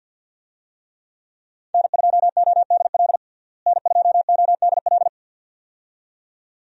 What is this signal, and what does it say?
Morse code, heard as a keyed tone, about 53 wpm, 700 Hz. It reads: N2ODL N2ODL
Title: USA Call Signs - 5 Characters - Repeated 2x - 50wpm